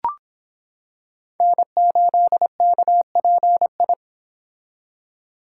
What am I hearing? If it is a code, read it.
Morse code, heard as a keyed tone, about 26 wpm, 700 Hz. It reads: N8KPI